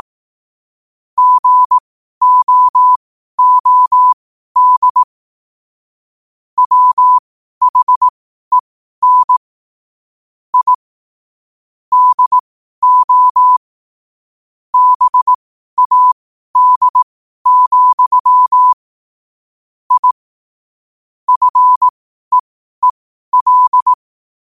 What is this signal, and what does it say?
Morse code, heard as a keyed tone, about 18 wpm, spaced as for 12 wpm, 1000 Hz. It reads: GOOD WHEN I DO BAD, I FEEL
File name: qotdfiles / 12wpm